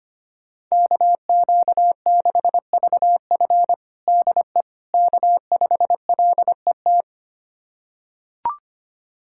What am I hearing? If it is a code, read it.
Morse code, heard as a keyed tone, about 25 wpm, 700 Hz. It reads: KQ6VF DE K5LET
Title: USA Callsign DE Callsign - 5 Characters - 25wpm